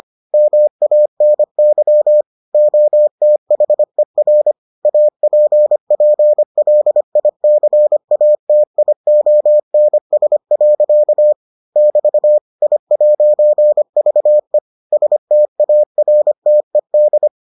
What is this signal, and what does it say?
Morse code, heard as a keyed tone, about 25 wpm, 600 Hz. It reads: MANY OTHER APPLICATIONS. <BT> I'VE STARTED